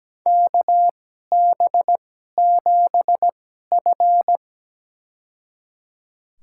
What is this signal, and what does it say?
Morse code, heard as a keyed tone, about 17 wpm, spaced as for 12 wpm, 700 Hz. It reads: KB7F
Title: USA Call Signs - 4 Characters - 2x Character Spacing - 17wpm